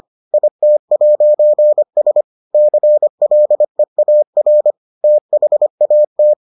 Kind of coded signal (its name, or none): Morse code